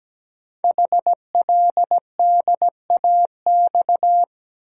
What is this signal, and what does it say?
Morse code, heard as a keyed tone, about 17 wpm, 700 Hz. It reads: HLDAX
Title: Code Groups - 5 Letters - 17wpm